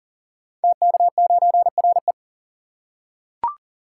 Morse code, heard as a keyed tone, about 38 wpm, 700 Hz. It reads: TK9RE
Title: Code Groups - 5 Letter:Numbers - 40wpm